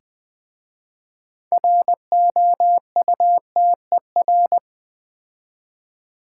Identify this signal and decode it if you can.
Morse code, heard as a keyed tone, about 20 wpm, 700 Hz. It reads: ROUTER
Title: Difficult Words with Hints - 20wpm